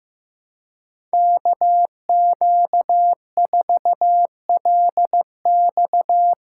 Morse code, heard as a keyed tone, about 15 wpm, 700 Hz. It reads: KQ4LX
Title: USA Call Signs - 5 Characters - 4x Character Spacing - 15wpm